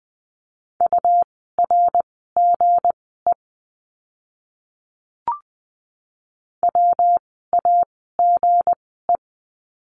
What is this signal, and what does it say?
Morse code, heard as a keyed tone, about 20 wpm, spaced as for 14 wpm, 700 Hz. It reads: URGE WAGE
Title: 160 Lesson 26f - Letter g - Words 20wpm